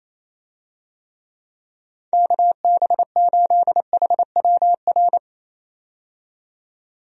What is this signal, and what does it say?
Morse code, heard as a keyed tone, about 28 wpm, 700 Hz. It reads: KB8HWR
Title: USA Call Signs - 6 Characters - 28wpm